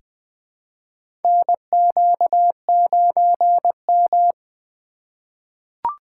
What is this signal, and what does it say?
Morse code, heard as a keyed tone, about 20 wpm, 700 Hz. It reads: NQ9M
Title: USA Call Signs - 4 Characters - 2x Character Spacing - 20wpm